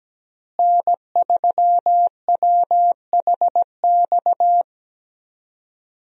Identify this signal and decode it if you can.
Morse code, heard as a keyed tone, about 17 wpm, 700 Hz. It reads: N3WHX
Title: USA Call Signs - 5 Characters - Repeated 3x - 17wpm